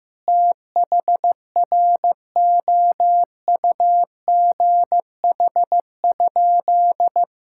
Morse code, heard as a keyed tone, about 15 wpm, 700 Hz. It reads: THROUGH?
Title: Sentences from Top 1000 Words - Review Part 6 - 15wpm